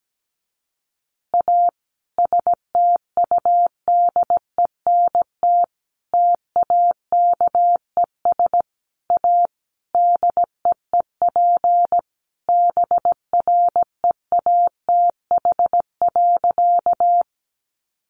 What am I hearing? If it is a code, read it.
Morse code, heard as a keyed tone, about 17 wpm, 700 Hz. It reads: A STUDENT TAKES A DEEP BREATH.
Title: Additional Easy Phrases 17wpm